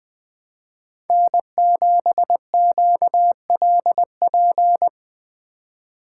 Morse code, heard as a keyed tone, about 19 wpm, 700 Hz. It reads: N7QLP
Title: Speed Racing - USA 1x3 Call Signs 17wpm